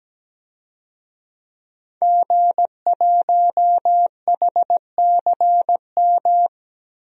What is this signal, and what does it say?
Morse code, heard as a keyed tone, about 17 wpm, 700 Hz. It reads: G1HCM
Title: DXCC-100 Call Signs - 5 Characters - 17wpm